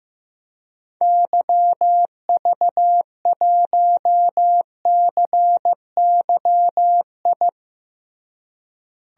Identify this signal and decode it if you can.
Morse code, heard as a keyed tone, about 15 wpm, 700 Hz. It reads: YV1CYI
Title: DXCC-100 Call Signs - 6 Characters - Repeated 2x - 15wpm